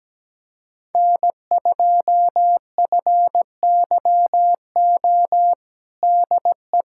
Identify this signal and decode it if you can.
Morse code, heard as a keyed tone, about 17 wpm, 700 Hz. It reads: N2FYO DE
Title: USA Callsign DE Callsign - 5 Characters - 17wpm